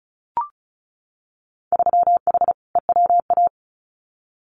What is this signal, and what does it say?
Morse code, heard as a keyed tone, about 36 wpm, 700 Hz. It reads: 3H EWA
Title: Winter Field Day Exchanges Encore 35wpm